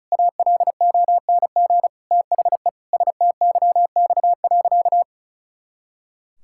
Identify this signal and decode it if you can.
Morse code, heard as a keyed tone, about 36 wpm, 700 Hz. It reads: ALONG THE STYX.
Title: Pangram Challenge v2 - 35wpm